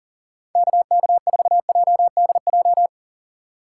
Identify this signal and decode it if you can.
Morse code, heard as a keyed tone, about 38 wpm, 700 Hz. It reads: KK4JDJ
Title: USA Call Signs - 4x Character Spacing - 40wpm